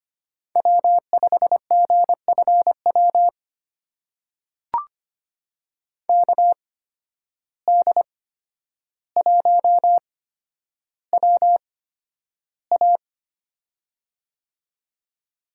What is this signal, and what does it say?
Morse code, heard as a keyed tone, about 25 wpm, spaced as for 11 wpm, 700 Hz. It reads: W5GFW KD1WA
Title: USA Call Signs - 5 Characters - 8x Character Spacing - 25wpm